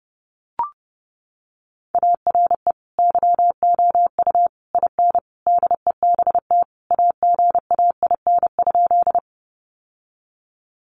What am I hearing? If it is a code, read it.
Morse code, heard as a keyed tone, about 28 wpm, 700 Hz. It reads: ARE YOU IN DEBT AGAIN?
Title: Sentences from Top 1300 Words 30wpm